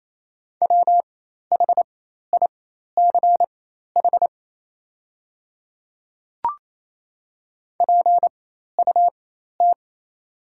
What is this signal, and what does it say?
Morse code, heard as a keyed tone, about 28 wpm, spaced as for 13 wpm, 700 Hz. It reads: WHICH PUT
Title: 4x Character Spacing - Top 300 Words - 28wpm